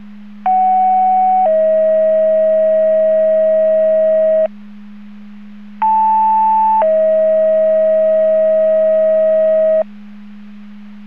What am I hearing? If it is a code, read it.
Two-tone page, heard as a one-second tone, then a three-second tone, about 750 then 640 Hz, about 880 then 640 Hz. It is heard from 0.5 to 9.8 s.